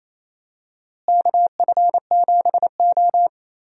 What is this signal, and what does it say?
Morse code, heard as a keyed tone, about 27 wpm, 700 Hz. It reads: KF7O
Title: USA Call Signs - 4 Characters - Repeated 3x - 28wpm